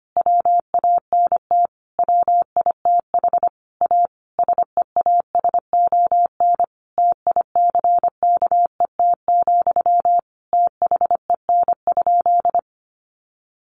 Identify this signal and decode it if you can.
Morse code, heard as a keyed tone, about 25 wpm, 700 Hz. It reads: WANT WITH A SEASON TICKET, THEN?
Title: Sentences from Top 1400 Words 25wpm